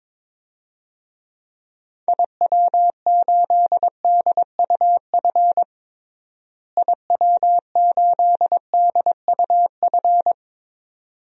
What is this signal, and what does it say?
Morse code, heard as a keyed tone, about 22 wpm, 700 Hz. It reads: IW8DUF IW8DUF
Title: DXCC-100 Call Signs - 6 Characters - Repeated 2x - 22wpm